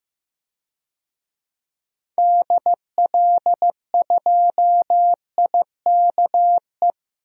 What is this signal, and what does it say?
Morse code, heard as a keyed tone, about 15 wpm, 700 Hz. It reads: DL2IKE
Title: DXCC-100 Call Signs - 6 Characters - 15wpm